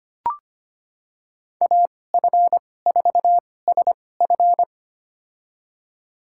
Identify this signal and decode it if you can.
Morse code, heard as a keyed tone, about 25 wpm, spaced as for 18 wpm, 700 Hz. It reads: AF4SF
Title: USA Call Signs - 5 Characters - 2x Character Spacing - 25wpm